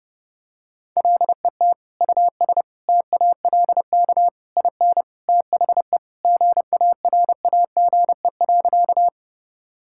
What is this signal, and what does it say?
Morse code, heard as a keyed tone, about 32 wpm, 700 Hz. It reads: LET US TALK IN THE GARAGE.
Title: Sentences from Top 1600 Words 30wpm